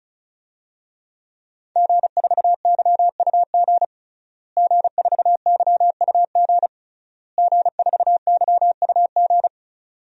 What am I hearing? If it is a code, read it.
Morse code, heard as a keyed tone, about 36 wpm, 700 Hz. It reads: G4YUG G4YUG G4YUG
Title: DXCC-100 Call Signs - 5 Characters - Repeated 3x - 35wpm